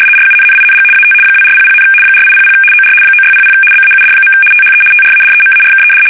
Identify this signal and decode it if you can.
RTTY, about 75 baud. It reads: RUNNING AT 2400, 4800 OR 9600 BAUD. THE TRADEMARKED
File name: fsk850-75B / teletype-corp